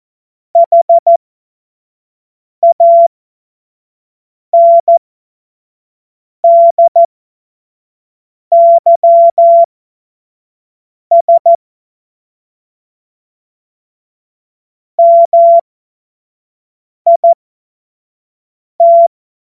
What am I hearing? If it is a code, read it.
Morse code, heard as a keyed tone, about 14 wpm, spaced as for 5 wpm, 680 Hz. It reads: HANDYS MIT
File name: Morsekurs / Witze / Witz 22. August 2023 25 F70